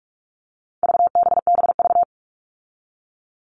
Morse code, heard as a keyed tone, about 44 wpm, 700 Hz. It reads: VBBV
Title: All Cut Numbers - 4-Digit 45wpm